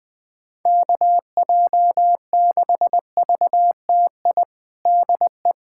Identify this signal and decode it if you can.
Morse code, heard as a keyed tone, about 20 wpm, 700 Hz. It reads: KJ6VTI DE
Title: USA Callsign DE Callsign - 6 Characters - 20wpm